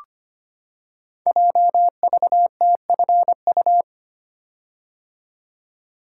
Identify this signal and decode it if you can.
Morse code, heard as a keyed tone, about 25 wpm, 700 Hz. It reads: JVTFU
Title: Code Groups - 5 Letter:Numbers - 25wpm